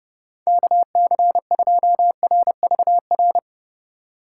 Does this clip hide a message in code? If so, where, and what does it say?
Yes: Morse code from 0.5 to 3.4 s, heard as a keyed tone, about 31 wpm, 700 Hz. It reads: KC2RVR